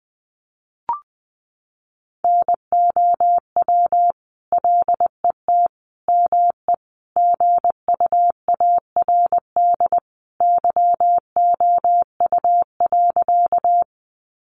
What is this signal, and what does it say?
Morse code, heard as a keyed tone, about 20 wpm, 700 Hz. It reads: NOW LET ME GUARD YOU.
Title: Sentences from Top 700 Words 20wpm